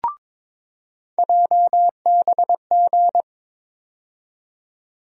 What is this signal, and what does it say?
Morse code, heard as a keyed tone, about 22 wpm, 700 Hz. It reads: JBG
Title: USA Call Sign Postfixes - 3 Letters 22wpm